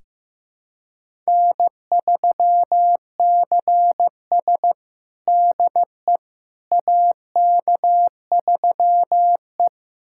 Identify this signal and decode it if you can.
Morse code, heard as a keyed tone, about 15 wpm, 700 Hz. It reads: N3CS DE AK3E
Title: USA Callsign DE Callsign - 4 Characters - 15wpm